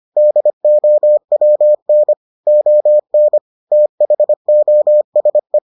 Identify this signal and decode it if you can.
Morse code, heard as a keyed tone, about 25 wpm, 600 Hz. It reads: DOWN ON THOSE